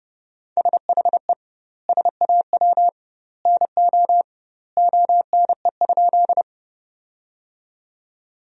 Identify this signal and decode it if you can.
Morse code, heard as a keyed tone, about 30 wpm, 700 Hz. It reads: SHE SAW NO ONE?
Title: 2x Word-Spacing - Sentences from Top 300 Words - 30wpm